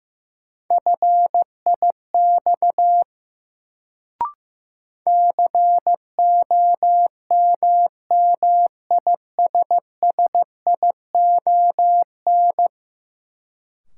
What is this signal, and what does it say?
Morse code, heard as a keyed tone, about 15 wpm, 700 Hz. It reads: FIX COMMISSION
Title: Sentences from Top 1400 Words - New Word Drill Alt - 15wpm